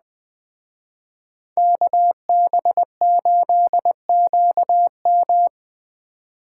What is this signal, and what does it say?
Morse code, heard as a keyed tone, about 19 wpm, 700 Hz. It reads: KB8QM